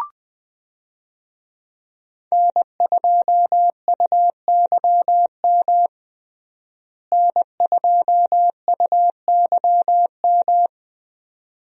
Morse code, heard as a keyed tone, about 20 wpm, 700 Hz. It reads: N2UYM N2UYM
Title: USA Call Signs - 5 Characters - Repeated 2x - 20wpm